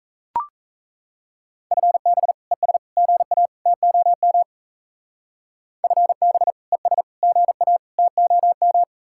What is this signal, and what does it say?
Morse code, heard as a keyed tone, about 34 wpm, 700 Hz. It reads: FB ES GA TOM FB ES GA TOM
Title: Speed-Racing Common QSO Phrases - 28wpm